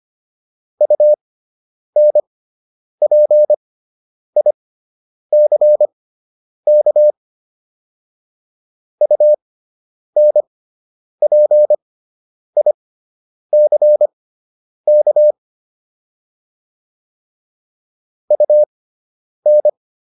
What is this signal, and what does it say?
Morse code, heard as a keyed tone, about 25 wpm, spaced as for 9 wpm, 600 Hz. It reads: UNPICK UNPICK UN